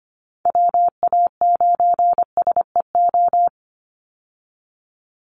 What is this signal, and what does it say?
Morse code, heard as a keyed tone, about 25 wpm, 700 Hz. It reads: WA9SEO
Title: Call Signs Repeated 2x - 25wpm